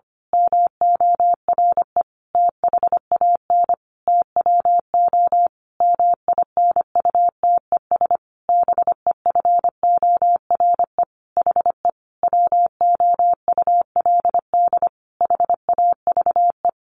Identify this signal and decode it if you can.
Morse code, heard as a keyed tone, about 25 wpm, 700 Hz. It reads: MORE THAN TWO MINUTES BEFORE HE WOULD HAVE
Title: A Sound Of Thunder 25wpm